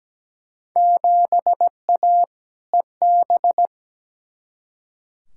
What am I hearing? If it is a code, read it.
Morse code, heard as a keyed tone, about 17 wpm, 700 Hz. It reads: 7A EB
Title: Field Day Exchanges Encore - 17wpm